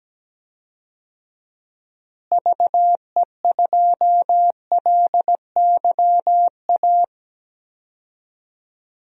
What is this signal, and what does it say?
Morse code, heard as a keyed tone, about 17 wpm, 700 Hz. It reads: VE2LYA
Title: DXCC-100 Call Signs - 6 Characters - Repeated 2x - 17wpm